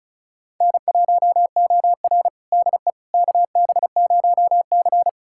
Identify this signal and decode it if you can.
Morse code, heard as a keyed tone, about 36 wpm, 700 Hz. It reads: N1OR DE KB0C
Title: USA Callsign DE Callsign - 4 Characters - 35wpm